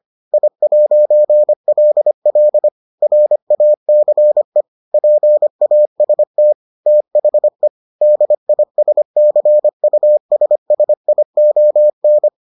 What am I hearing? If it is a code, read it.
Morse code, heard as a keyed tone, about 25 wpm, 600 Hz. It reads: I'LL RACE PAST THE DISCUSSION